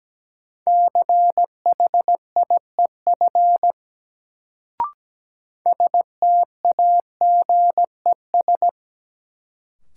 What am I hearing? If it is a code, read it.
Morse code, heard as a keyed tone, about 17 wpm, 700 Hz. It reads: CHIEF STAGES
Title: Sets of 5 Words Encore - New Word Drill - 17wpm